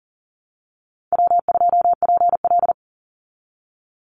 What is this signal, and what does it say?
Morse code, heard as a keyed tone, about 37 wpm, 700 Hz. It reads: W2PL
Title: Call Signs Repeated 3x - 40wpm